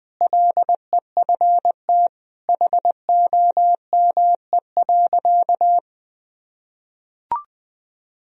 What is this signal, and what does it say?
Morse code, heard as a keyed tone, about 20 wpm, 700 Hz. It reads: LEFT HOME.
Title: Sentences from Top 1000 Words - Review Part 4 - 20wpm